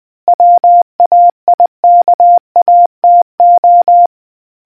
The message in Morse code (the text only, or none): WAIKATO